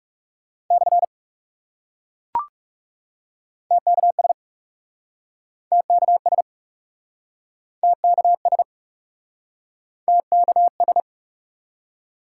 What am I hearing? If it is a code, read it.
Morse code, heard as a keyed tone, about 38 wpm, 700 Hz. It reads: / TKS TKS TKS TKS